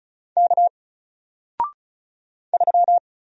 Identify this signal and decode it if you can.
Morse code, heard as a keyed tone, about 36 wpm, 700 Hz. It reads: K 3
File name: Single-Letter-Number-35wpm